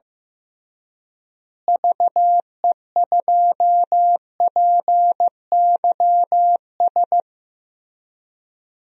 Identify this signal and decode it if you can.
Morse code, heard as a keyed tone, about 15 wpm, 700 Hz. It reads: VE2PYS
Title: DXCC-100 Call Signs - 6 Characters - Repeated 3x - 15wpm